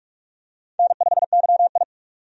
Morse code, heard as a keyed tone, about 46 wpm, 700 Hz. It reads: N5YI
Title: USA Call Signs - 4 Characters - 45wpm